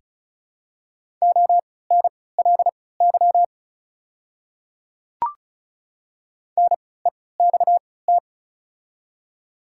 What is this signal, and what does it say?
Morse code, heard as a keyed tone, about 36 wpm, spaced as for 20 wpm, 700 Hz. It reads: ONLY NEXT
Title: 3x Character Spacing - Top 300 Words - 35wpm